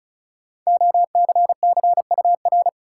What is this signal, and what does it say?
Morse code, heard as a keyed tone, about 34 wpm, 700 Hz. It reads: OCCUR
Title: Sentences from Top 900 Words - New Word Drill - 35wpm